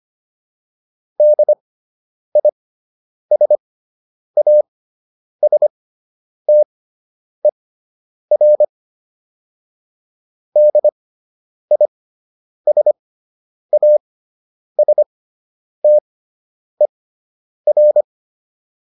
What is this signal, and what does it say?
Morse code, heard as a keyed tone, about 25 wpm, spaced as for 9 wpm, 600 Hz. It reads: DISASTER DISASTER